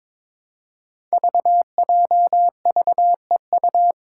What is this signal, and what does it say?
Morse code, heard as a keyed tone, about 21 wpm, 700 Hz. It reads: VJVEU
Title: Five Challenging Letter-Number Combinations 22wpm